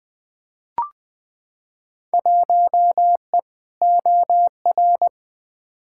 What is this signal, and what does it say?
Morse code, heard as a keyed tone, about 20 wpm, 700 Hz. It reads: 1E OR
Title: Field Day Exchanges - 20wpm